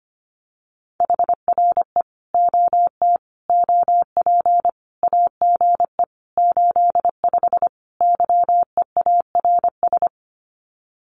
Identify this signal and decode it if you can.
Morse code, heard as a keyed tone, about 25 wpm, 700 Hz. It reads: HRE OT OP AGE 85 YEARS
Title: Common QSO Phrases 25wpm